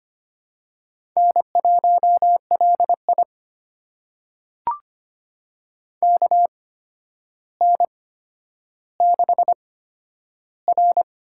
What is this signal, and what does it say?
Morse code, heard as a keyed tone, about 25 wpm, spaced as for 11 wpm, 700 Hz. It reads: N1LI KN6R
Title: USA Call Signs - 4 Characters - 8x Character Spacing - 25wpm